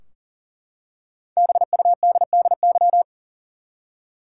Morse code, heard as a keyed tone, about 40 wpm, 700 Hz. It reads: BUDDY
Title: Five Letter Words - 40wpm